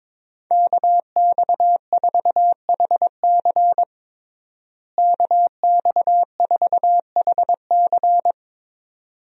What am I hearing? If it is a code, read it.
Morse code, heard as a keyed tone, about 22 wpm, 700 Hz. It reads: KX4HC KX4HC
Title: USA Call Signs - 5 Characters - Repeated 2x - 22wpm